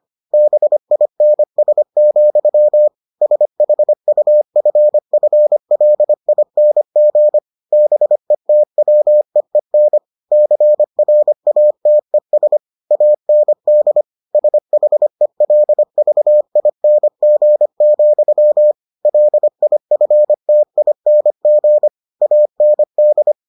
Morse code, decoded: BINS, SHUFFLING BETWEEN CRATES AND SHELVING, LIFTING AND

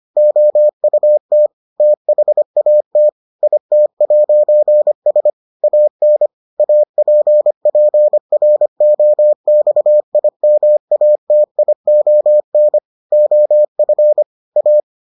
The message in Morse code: OUT THAT IT'S AN APPROXIMATION OF A

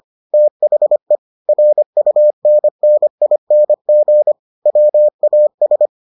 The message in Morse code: THE RUNNING WAS

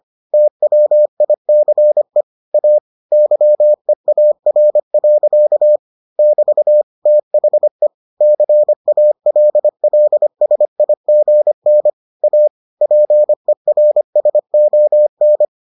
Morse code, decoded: TWICE A YEAR. <BT> THE CALLSIGN A PERSON

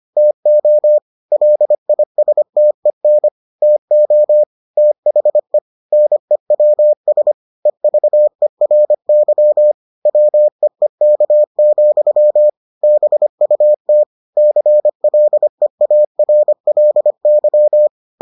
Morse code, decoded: TO LISTEN TO THE NEWS EVERY WEEK, BUT CLEARLY

